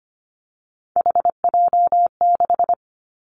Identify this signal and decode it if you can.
Morse code, heard as a keyed tone, about 25 wpm, 700 Hz. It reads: HJ6
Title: Callsign Prefixes for Top 100 DXCC - 25wpm